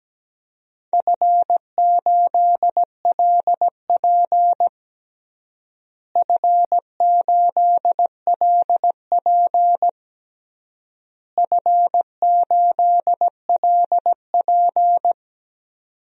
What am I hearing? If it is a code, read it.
Morse code, heard as a keyed tone, about 17 wpm, 700 Hz. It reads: F8LP F8LP F8LP
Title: DXCC-100 Call Signs - 4 Characters - Repeated 3x - 17wpm